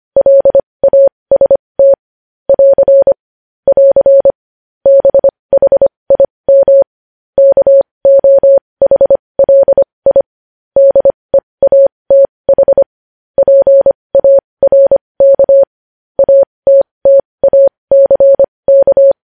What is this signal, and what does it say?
Morse code, heard as a keyed tone, about 25 wpm, spaced as for 20 wpm, 550 Hz. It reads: LAST <AR> <AR> BHIM KOHLI DEATH PARK ATTACK
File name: podcast / News headlines-1x-25x20